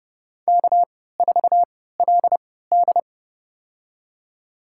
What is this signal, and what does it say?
Morse code, heard as a keyed tone, about 30 wpm, spaced as for 17 wpm, 700 Hz. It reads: K4LD
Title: USA Call Signs - 3x Character Spacing - 30wpm